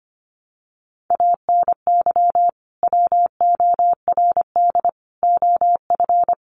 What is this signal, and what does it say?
Morse code, heard as a keyed tone, about 25 wpm, 700 Hz. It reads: ANY WORD OF
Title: Sentences from Top 1300 Words 25wpm